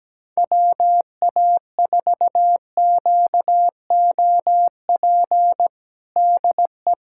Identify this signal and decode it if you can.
Morse code, heard as a keyed tone, about 17 wpm, 700 Hz. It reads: WA4QOP DE
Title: USA Callsign DE Callsign 17wpm